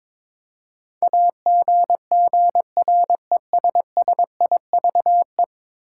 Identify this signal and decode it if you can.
Morse code, heard as a keyed tone, about 22 wpm, 700 Hz. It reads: AGGRESSIVE
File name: practice / 22wpm / 10 Letter Words - First 3 Letters Given - 22wpm